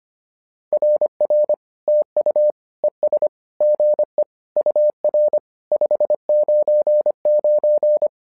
Morse code, heard as a keyed tone, about 25 wpm, 600 Hz. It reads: RR TU ES GE UR 599
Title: SOTA QSO Practice 25wpm